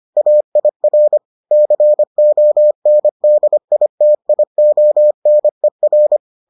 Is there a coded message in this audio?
Yes: Morse code from 0.2 to 6.2 s, heard as a keyed tone, about 25 wpm, 600 Hz. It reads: AIR CONDITIONER